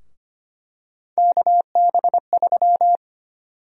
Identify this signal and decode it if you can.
Morse code, heard as a keyed tone, about 25 wpm, 700 Hz. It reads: KB3
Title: USA Call Sign Prefixes - 25wpm